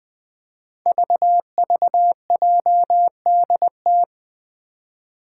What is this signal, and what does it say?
Morse code, heard as a keyed tone, about 20 wpm, 700 Hz. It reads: VVJDT
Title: Five Challenging Letter-Number Combinations 20wpm